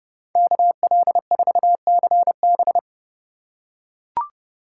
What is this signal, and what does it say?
Morse code, heard as a keyed tone, about 31 wpm, 700 Hz. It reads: KL4CB